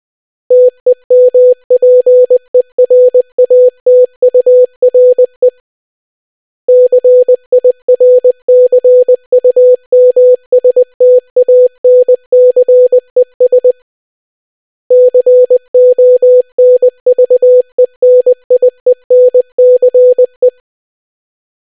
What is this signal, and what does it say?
Morse code, heard as a keyed tone, about 20 wpm, 500 Hz. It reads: TEMPERATURE CIRCUMSTANCES CONVENIENCE